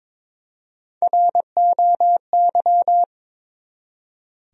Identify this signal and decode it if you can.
Morse code, heard as a keyed tone, about 22 wpm, 700 Hz. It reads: ROY
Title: USA Common Names - Max 3 Letters - 22wpm